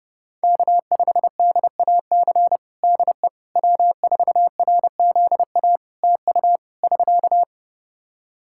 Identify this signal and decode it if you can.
Morse code, heard as a keyed tone, about 30 wpm, 700 Hz. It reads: K5DAC DE W4RZA TU <SK>
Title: Practice QSO - Exchange 4 - Ending - 30wpm